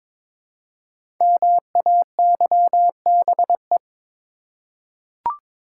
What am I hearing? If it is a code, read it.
Morse code, heard as a keyed tone, about 23 wpm, 700 Hz. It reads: MAYBE
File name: 5 Letter Words - First 3 Letters Given - 22wpm